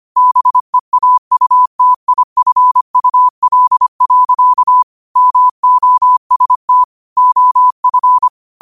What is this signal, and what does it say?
Morse code, heard as a keyed tone, about 25 wpm, 1000 Hz. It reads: BEAUTIFUL. MOST OF
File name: qotdfiles / 25wpm